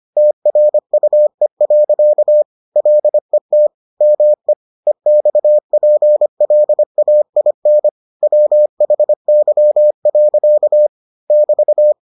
Morse code, heard as a keyed tone, about 25 wpm, 600 Hz. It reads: TRUE. LET ME EXPLAIN WHY. <BT>